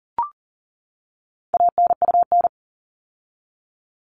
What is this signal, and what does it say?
Morse code, heard as a keyed tone, about 40 wpm, 700 Hz. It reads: ANUN
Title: All Cut Numbers - 4-Digit 40wpm